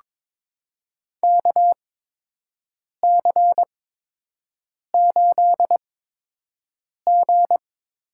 Morse code, heard as a keyed tone, about 22 wpm, spaced as for 6 wpm, 700 Hz. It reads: KC8G
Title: USA Call Signs - 6 Characters - 8x Character Spacing - 22wpm